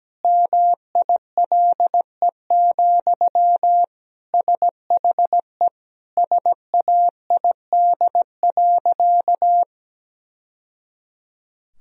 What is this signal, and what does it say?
Morse code, heard as a keyed tone, about 17 wpm, 700 Hz. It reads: MILE, SHE SAID.
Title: Sentences from Top 1000 Words - Review Part 1 - 17wpm